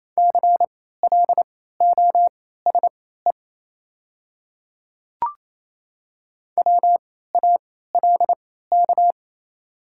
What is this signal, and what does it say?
Morse code, heard as a keyed tone, about 28 wpm, spaced as for 16 wpm, 700 Hz. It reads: CLOSE WALK
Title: 3x Character Spacing - Top 300 Words - 28wpm